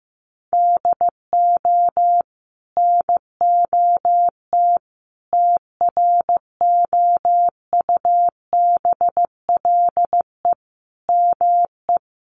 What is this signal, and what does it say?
Morse code, heard as a keyed tone, about 15 wpm, 700 Hz. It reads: DO NOT TROUBLE ME
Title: Sentences from Top 1200 Words 15wpm